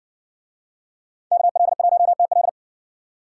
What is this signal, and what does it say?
Morse code, heard as a keyed tone, about 62 wpm, 700 Hz. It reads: KL1TL